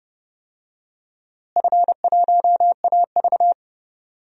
Morse code, heard as a keyed tone, about 30 wpm, 700 Hz. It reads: F1AV
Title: DXCC-100 Call Signs - 4 Characters - Repeated 3x - 30wpm